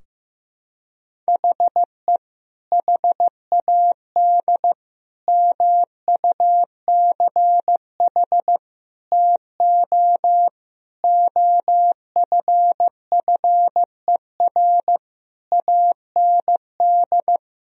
Morse code, heard as a keyed tone, about 16 wpm, 700 Hz. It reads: HE HAD MUCH TO OFFER AND
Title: Sentences from Top 1000 Words - Review Part 4 - 15wpm